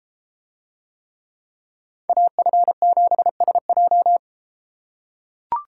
Morse code, heard as a keyed tone, about 31 wpm, 700 Hz. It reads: AF7SJ